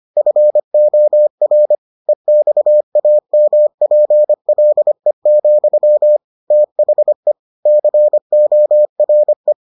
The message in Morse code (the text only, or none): FOR EXAMPLE, THE CORE